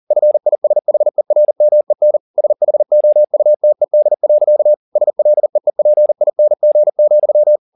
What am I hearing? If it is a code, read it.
Morse code, heard as a keyed tone, about 40 wpm, 600 Hz. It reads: FISHERMEN SHOUTED. SLEEPING,
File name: the old man and the sea 2 40wpm